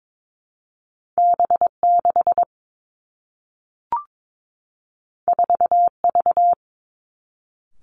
Morse code, heard as a keyed tone, about 22 wpm, 700 Hz. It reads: B 6 4 V
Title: Two Challenging Letter-Number Combinations 22wpm